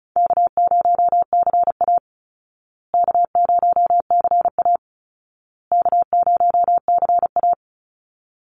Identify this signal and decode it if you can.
Morse code, heard as a keyed tone, about 34 wpm, 700 Hz. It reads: K0CA K0CA K0CA
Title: Call Signs Repeated 2x - 35wpm